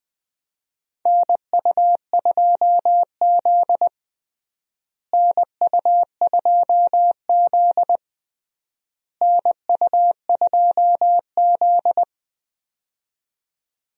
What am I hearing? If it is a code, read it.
Morse code, heard as a keyed tone, about 21 wpm, 700 Hz. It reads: NU2Z NU2Z NU2Z
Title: USA Call Signs - 4 Characters - Repeated 3x - 20wpm